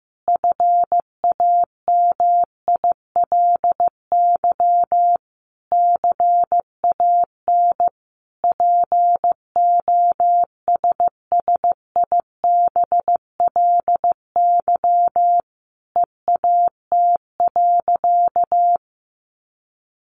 Morse code, heard as a keyed tone, about 15 wpm, 700 Hz. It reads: FAMILY CAN POSSIBLY EAT.
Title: Sentences from Top 1300 Words 15wpm